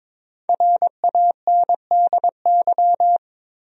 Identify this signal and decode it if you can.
Morse code, heard as a keyed tone, about 23 wpm, 700 Hz. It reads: RANDY
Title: USA Common Names - Max 5 Letters - 22wpm